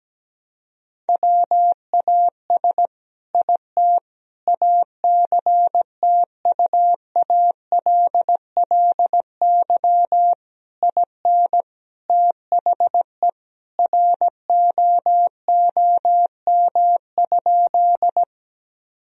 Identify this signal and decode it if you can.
Morse code, heard as a keyed tone, about 17 wpm, 700 Hz. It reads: WAS IT ACTUALLY IN THE ROOM?
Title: Sentences from Top 1000 Words - Review Part 4 - 17wpm